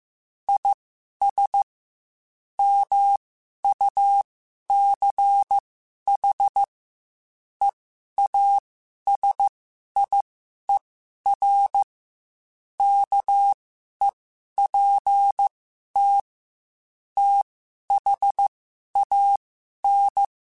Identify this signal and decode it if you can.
Morse code, heard as a keyed tone, about 15 wpm, spaced as for 11 wpm, 800 Hz. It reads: IS MUCH EASIER KEPT THAN
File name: Quotes / Quotes-mp3s / Character @13